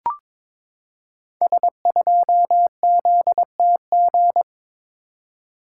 Morse code, heard as a keyed tone, about 22 wpm, 700 Hz. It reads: S2ZTG
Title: Code Groups - 5 Letter:Numbers - 22wpm